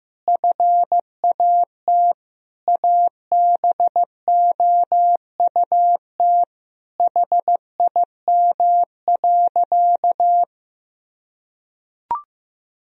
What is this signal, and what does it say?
Morse code, heard as a keyed tone, about 16 wpm, 700 Hz. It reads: FAT ABOUT HIM.
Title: Sentences from Top 1000 Words - Review Part 4 - 15wpm